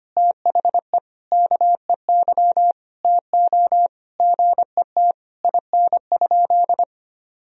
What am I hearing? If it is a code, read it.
Morse code, heard as a keyed tone, about 25 wpm, 700 Hz. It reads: THE KEY TO GET IN?
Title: Sentences from Top 1000 Words - Review Part 4 - 25wpm